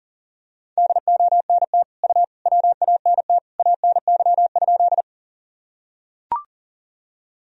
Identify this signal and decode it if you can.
Morse code, heard as a keyed tone, about 40 wpm, 700 Hz. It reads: DONT U WANT ANY?
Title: Sentences from Top 300 Words and Abbreviations 40wpm